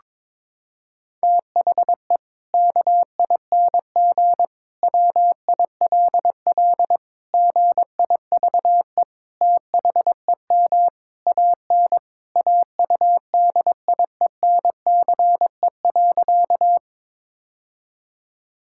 Morse code, heard as a keyed tone, about 22 wpm, 700 Hz. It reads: THE KING WILL GIVE THEM AN AUDIENCE.
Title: Sentences from Top 1000 Words - Review Part 5 - 22wpm